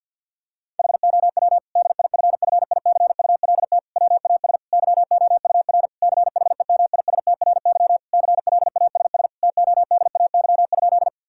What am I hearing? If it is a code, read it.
Morse code, heard as a keyed tone, about 50 wpm, 700 Hz. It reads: HOW DIFFICULT WAS YOUR CHEMISTRY CLASS TODAY?
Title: Sentences from Top 1000 Words - Review Part 6 - 50wpm